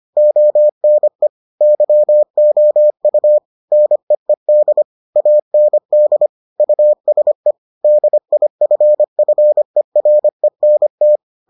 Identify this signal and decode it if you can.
Morse code, heard as a keyed tone, about 25 wpm, 600 Hz. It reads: ONE YOU NEED AND USE DIFFERENT